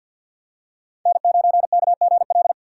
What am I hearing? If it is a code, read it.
Morse code, heard as a keyed tone, about 50 wpm, 700 Hz. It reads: N9KGL
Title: USA Call Signs - 3x Character Spacing - 50wpm